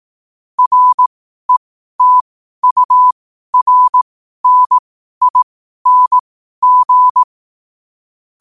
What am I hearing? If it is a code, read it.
Morse code, heard as a keyed tone, about 18 wpm, spaced as for 12 wpm, 1000 Hz. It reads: RETURNING
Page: New quotes every day in morse code at 12 Words per minute.